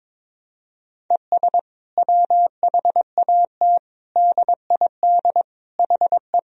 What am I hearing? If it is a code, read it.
Morse code, heard as a keyed tone, about 22 wpm, 700 Hz. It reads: ES WHAT DID HE
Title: Sentences from Top 200 Words and Abbreviations 22wpm